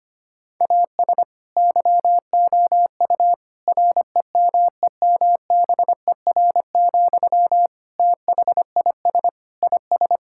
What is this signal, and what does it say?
Morse code, heard as a keyed tone, about 25 wpm, 700 Hz. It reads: AS YOU REMEMBER, THIS IS
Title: Sentences from Top 1000 Words - Review Part 5 - 25wpm